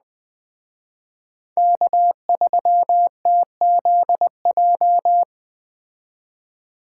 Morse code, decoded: K3TZJ

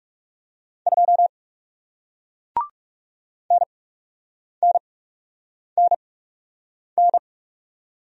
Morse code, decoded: 2 NNNN

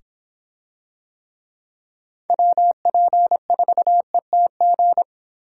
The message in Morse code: WP4ETG

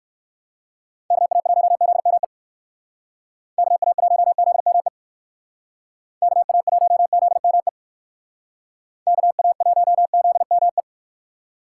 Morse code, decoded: KA1ZME KA1ZME KA1ZME KA1ZME

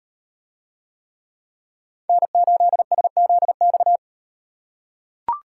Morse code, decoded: N8SZX

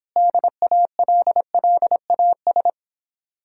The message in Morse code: DALLAS

